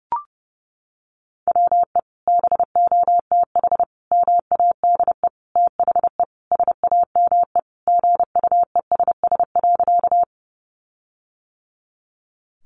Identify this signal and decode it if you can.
Morse code, heard as a keyed tone, about 30 wpm, 700 Hz. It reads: WE BOTH MADE THE SAME GUESS.
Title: Sentences from Top 1500 Words 30wpm